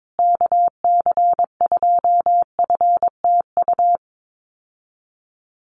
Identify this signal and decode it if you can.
Morse code, heard as a keyed tone, about 22 wpm, 700 Hz. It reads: KC2FTU
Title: USA Call Signs Repeated 2x 22wpm